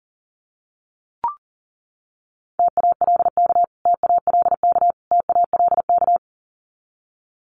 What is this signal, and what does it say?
Morse code, heard as a keyed tone, about 40 wpm, 700 Hz. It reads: TALK TALK TALK
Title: Top 500 Words Repeated 40wpm